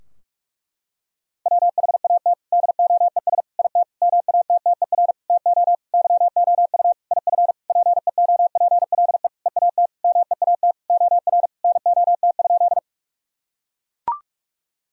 Morse code, heard as a keyed tone, about 41 wpm, 700 Hz. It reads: WHAT DOES IT MATTER TO YOU IF PEOPLE EAT MEAT OR NOT?
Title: Sentences from Top 1000 Words - Review Part 3 - 45wpm